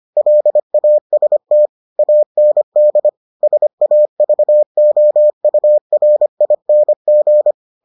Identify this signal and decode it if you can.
Morse code, heard as a keyed tone, about 25 wpm, 600 Hz. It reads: LAST AND SAVOURING